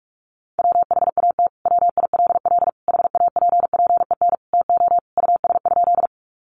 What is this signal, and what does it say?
Morse code, heard as a keyed tone, about 46 wpm, 700 Hz. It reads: WHAT WILL HAPPEN TO US?
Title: Sentences from Top 400 Words 45wpm